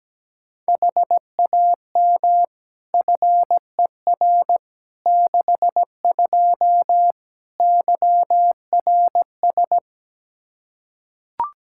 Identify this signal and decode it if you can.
Morse code, heard as a keyed tone, about 17 wpm, 700 Hz. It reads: HAM FER 62 YRS
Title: K1USN SST QSO Practice 17wpm